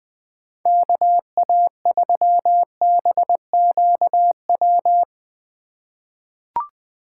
Morse code, heard as a keyed tone, about 20 wpm, 700 Hz. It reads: KA3BQW
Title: USA Call Signs - 6 Characters - 4x Character Spacing - 20wpm